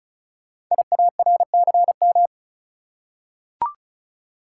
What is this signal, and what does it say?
Morse code, heard as a keyed tone, about 36 wpm, 690 Hz. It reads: IARCM
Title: Code Groups - 5 Letters - 35wpm